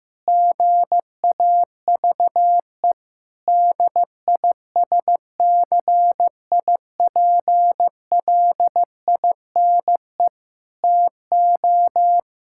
Morse code, heard as a keyed tone, about 15 wpm, 700 Hz. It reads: GAVE DISCIPLINE TO
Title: Sentences from Top 1000 Words - Review Part 6 - 15wpm